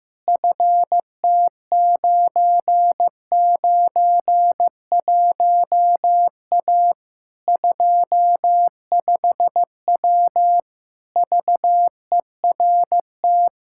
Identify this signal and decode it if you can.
Morse code, heard as a keyed tone, about 15 wpm, 700 Hz. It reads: FT991A 25W VERT
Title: Practice QSO - Exchange 2 - Info - 15wpm